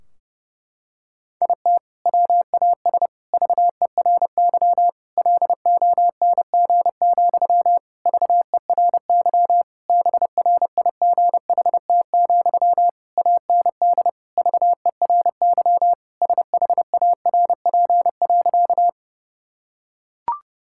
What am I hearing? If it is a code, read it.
Morse code, heard as a keyed tone, about 30 wpm, 700 Hz. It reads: IT WAS VERY LONG, VERY BRIGHT, AND VERY SHARP.
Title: Sentences from Top 1600 Words 30wpm